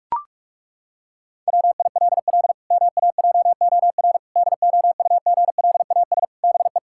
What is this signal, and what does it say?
Morse code, heard as a keyed tone, about 41 wpm, 700 Hz. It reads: WILL MAJOR DOUGLAS BE
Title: Pangram Challenge v2 - 45wpm